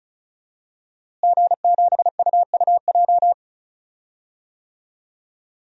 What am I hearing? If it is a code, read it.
Morse code, heard as a keyed tone, about 34 wpm, 700 Hz. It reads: G7UUJ
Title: DXCC-100 Call Signs - 5 Characters - Repeated 2x - 35wpm